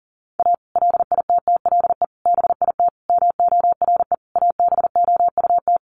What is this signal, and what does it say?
Morse code, heard as a keyed tone, about 40 wpm, 700 Hz. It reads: A LITTLE BIT MORE ABOUT
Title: Sets of 5 Words Encore2 40wpm